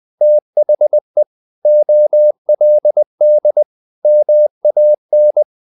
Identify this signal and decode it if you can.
Morse code, heard as a keyed tone, about 20 wpm, 600 Hz. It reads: THE OLD MAN
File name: the old man and the sea / the old man and the sea part 1 20wpm